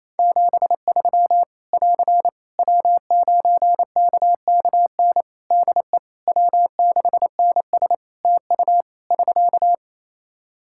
Morse code, heard as a keyed tone, about 27 wpm, 700 Hz. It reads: 73 <AR> W9KKN DE W6NS TU <SK>
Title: Practice QSO - Exchange 4 - Ending - 28wpm